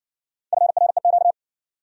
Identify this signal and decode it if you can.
Morse code, heard as a keyed tone, about 60 wpm, 700 Hz. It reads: FREQ